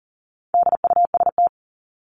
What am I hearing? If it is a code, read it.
Morse code, heard as a keyed tone, about 40 wpm, 700 Hz. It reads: DUST
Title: Top 1751-2000 Words 40wpm